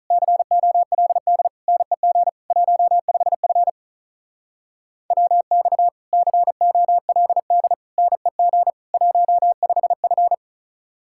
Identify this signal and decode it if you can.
Morse code, heard as a keyed tone, about 43 wpm, 700 Hz. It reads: COLD NEG 15F WX COLD NEG 15F
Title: Speed-Racing Common QSO Phrases Encore - 35wpm